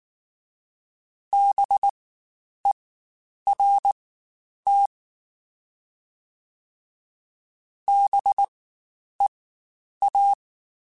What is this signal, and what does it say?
Morse code, heard as a keyed tone, about 19 wpm, spaced as for 9 wpm, 800 Hz. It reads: BERT BEA
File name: Spaced Alphas / Spaced Alphas mp3s / Bs sp @18